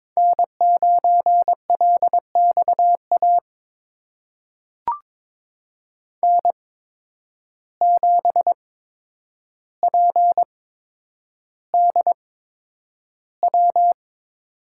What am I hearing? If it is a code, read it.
Morse code, heard as a keyed tone, about 22 wpm, spaced as for 9 wpm, 700 Hz. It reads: N9LXA N7PDW
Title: USA Call Signs - 5 Characters - 8x Character Spacing - 22wpm